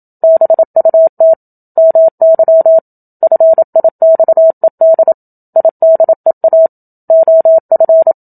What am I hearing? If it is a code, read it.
Morse code, heard as a keyed tone, about 27 wpm, 650 Hz. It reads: BUT MY FIXED IDEA OF
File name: Book Two / 01-Chapter 1 - 27 WPM